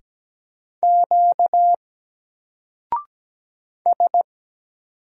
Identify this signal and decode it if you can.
Morse code, heard as a keyed tone, about 17 wpm, 700 Hz. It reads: Q S
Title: Single Letter-Number 17wpm